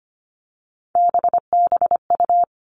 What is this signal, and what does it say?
Morse code, heard as a keyed tone, about 25 wpm, 700 Hz. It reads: BBU
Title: Three Challenging Letter-Number Combinations 25wpm